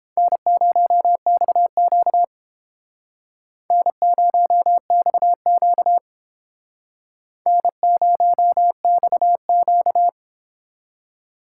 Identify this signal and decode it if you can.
Morse code, heard as a keyed tone, about 30 wpm, 700 Hz. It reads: N0XQ N0XQ N0XQ